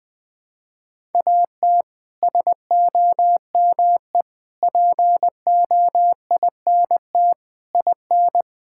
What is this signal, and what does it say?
Morse code, heard as a keyed tone, about 21 wpm, 700 Hz. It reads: AT SOME POINT IN
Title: Sets of 6 Words - 20wpm